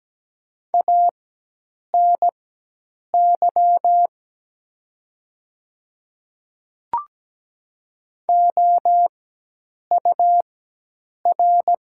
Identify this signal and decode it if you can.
Morse code, heard as a keyed tone, about 17 wpm, spaced as for 8 wpm, 700 Hz. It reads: ANY OUR
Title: 4x Character Spacing - Top 200 Words - 17wpm